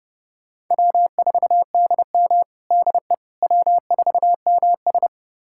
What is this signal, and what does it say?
Morse code, heard as a keyed tone, about 30 wpm, 700 Hz. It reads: W4DM DE W4MS
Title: USA Callsign DE Callsign - 4 Characters - 30wpm